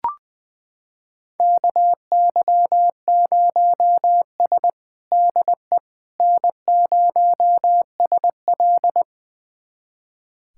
Morse code, heard as a keyed tone, about 20 wpm, 700 Hz. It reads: KY0S DE N0SL
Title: USA Callsign DE Callsign - 4 Characters - 20wpm